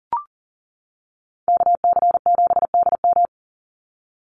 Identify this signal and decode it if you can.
Morse code, heard as a keyed tone, about 40 wpm, 700 Hz. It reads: KC7DM
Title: NA SOTA Activator Callsigns 40wpm